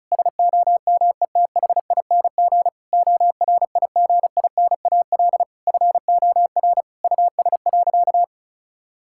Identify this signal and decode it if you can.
Morse code, heard as a keyed tone, about 36 wpm, 700 Hz. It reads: SOMETHING ORIGINAL FOR US.
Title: Sentences from Top 1000 Words - Review Part 6 - 35wpm